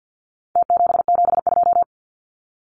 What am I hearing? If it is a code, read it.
Morse code, heard as a keyed tone, about 50 wpm, 700 Hz. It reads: T772
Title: Numbers - Contest Style - 4-Digit 50wpm